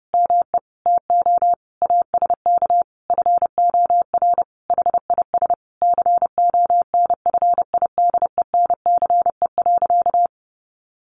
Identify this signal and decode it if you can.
Morse code, heard as a keyed tone, about 30 wpm, 700 Hz. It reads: ME TO ASK FOR HIS CONFIDENCE.
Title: Sentences from Top 1500 Words 30wpm